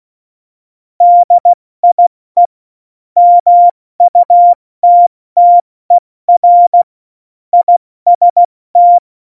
Morse code, heard as a keyed tone, about 16 wpm, 700 Hz. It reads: DIE MUTTER IST
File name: Morsekurs / Witze / Witz 08. August 17, Tempo 70 F80